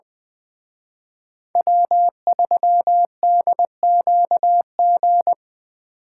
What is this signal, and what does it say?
Morse code, heard as a keyed tone, about 24 wpm, 700 Hz. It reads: W3DQG
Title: Speed Racing - USA 1x3 Call Signs 20wpm